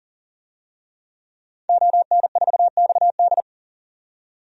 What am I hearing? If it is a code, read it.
Morse code, heard as a keyed tone, about 40 wpm, 700 Hz. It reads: ON4XD
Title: DXCC-100 Call Signs - 5 Characters - 40wpm